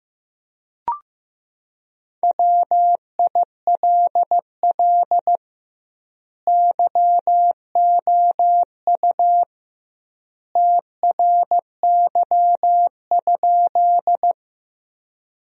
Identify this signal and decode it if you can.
Morse code, heard as a keyed tone, about 15 wpm, 700 Hz. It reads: WILL YOU TRY?
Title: 2x Word-Spacing - Sentences from Top 200 Words - 15wpm